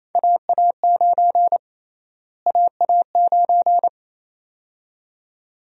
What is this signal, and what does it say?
Morse code, heard as a keyed tone, about 27 wpm, 700 Hz. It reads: AA9 AA9
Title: USA Call Sign Prefixes - Repeated 2x - 28wpm